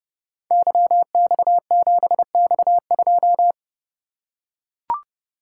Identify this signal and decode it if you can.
Morse code, heard as a keyed tone, about 30 wpm, 700 Hz. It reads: YX7X2
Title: Code Groups - 5 Letter:Numbers - 30wpm